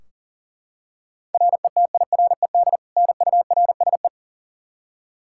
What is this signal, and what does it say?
Morse code, heard as a keyed tone, about 40 wpm, 700 Hz. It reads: RETIRED NURSE
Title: Practice QSO - Exchange 3 - More Info - 40wpm